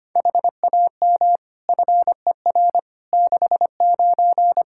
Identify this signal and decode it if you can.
Morse code, heard as a keyed tone, about 25 wpm, 700 Hz. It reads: HAM FER 69
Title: K1USN SST QSO Practice 25wpm